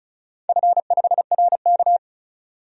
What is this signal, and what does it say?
Morse code, heard as a keyed tone, about 36 wpm, 690 Hz. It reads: F5RK
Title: DXCC-100 Call Signs - 4 Characters - 35wpm